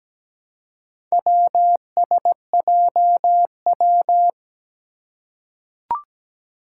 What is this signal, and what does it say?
Morse code, heard as a keyed tone, about 17 wpm, 700 Hz. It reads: WSJW
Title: Four Challenging Letter-Number Combinations 17wpm